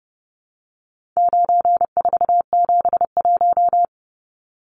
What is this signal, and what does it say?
Morse code, heard as a keyed tone, about 30 wpm, 700 Hz. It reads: 9471
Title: Numbers - 4-Digit 30wpm